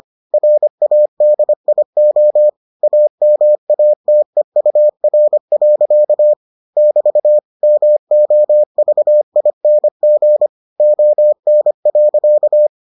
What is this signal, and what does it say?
Morse code, heard as a keyed tone, about 25 wpm, 600 Hz. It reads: RADIO AMATEUR. <BT> MOVING ON.